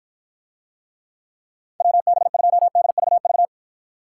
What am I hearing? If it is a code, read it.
Morse code, heard as a keyed tone, about 50 wpm, 700 Hz. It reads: WB2DVV